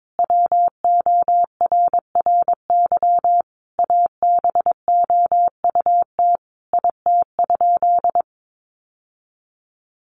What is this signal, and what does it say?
Morse code, heard as a keyed tone, about 22 wpm, 700 Hz. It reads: WORRY ABOUT IT?
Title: Sentences from Top 1100 Words 22wpm